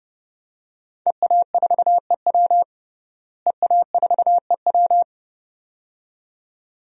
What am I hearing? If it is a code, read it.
Morse code, heard as a keyed tone, about 32 wpm, 700 Hz. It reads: EA4EW EA4EW
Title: DXCC-100 Call Signs - 5 Characters - Repeated 2x - 30wpm